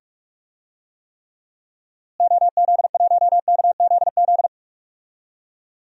Morse code, heard as a keyed tone, about 46 wpm, 700 Hz. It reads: OZ1KZZ
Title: DXCC-100 Call Signs - 6 Characters - Repeated 2x - 45wpm